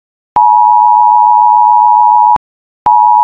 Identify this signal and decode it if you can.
Emergency Alert System (SAME) tones (the 853 and 960 Hz attention tone). It is heard from 0.3 to 2.4 s.